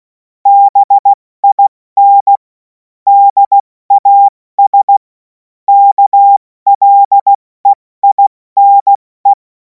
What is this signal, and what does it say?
Morse code, heard as a keyed tone, about 16 wpm, 800 Hz. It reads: BIN DAS KLEINE
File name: Morsekurs / Witze / Witz 03. März 2026 70 F80